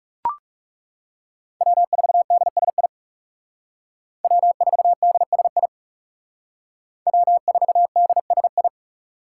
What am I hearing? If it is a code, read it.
Morse code, heard as a keyed tone, about 36 wpm, 700 Hz. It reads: W4DSI W4DSI W4DSI
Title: Speed Racing - USA Call Signs 30wpm